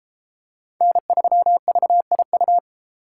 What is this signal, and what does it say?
Morse code, heard as a keyed tone, about 32 wpm, 700 Hz. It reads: N3VIU